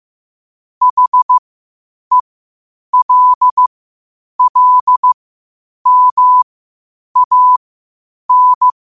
Morse code, heard as a keyed tone, about 15 wpm, spaced as for 8 wpm, 1000 Hz. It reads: HELLMAN
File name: qotdfiles / 8wpm